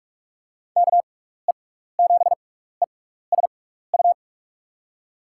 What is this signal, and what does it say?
Morse code, heard as a keyed tone, about 46 wpm, 700 Hz. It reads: K E 7 E S U
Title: USA Call Signs - 6 Characters - 6x Character Spacing - 45wpm